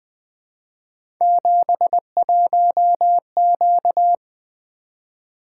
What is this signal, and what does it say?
Morse code, heard as a keyed tone, about 20 wpm, 700 Hz. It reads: 71Q
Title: Three Letter-Number Combinations - 20wpm